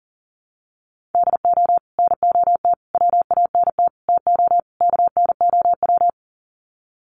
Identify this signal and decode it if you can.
Morse code, heard as a keyed tone, about 40 wpm, 700 Hz. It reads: DO NOT WANT TO KNOW
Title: Sets of 5 Words Encore 40wpm